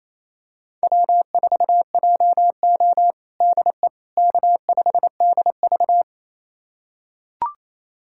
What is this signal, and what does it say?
Morse code, heard as a keyed tone, about 28 wpm, 700 Hz. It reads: W4JO DE K5DV
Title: USA Callsign DE Callsign - 4 Characters - 28wpm